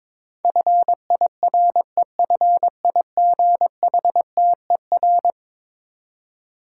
Morse code, heard as a keyed tone, about 22 wpm, 700 Hz. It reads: FIREFIGHTER
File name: practice / 22wpm / Eleven Letter Words - 22wpm